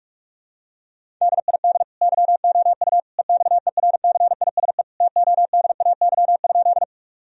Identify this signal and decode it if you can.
Morse code, heard as a keyed tone, about 41 wpm, 700 Hz. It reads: DID YOU EXERCISE TODAY?
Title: Sentences from Top 1000 Words - Review Part 4 - 45wpm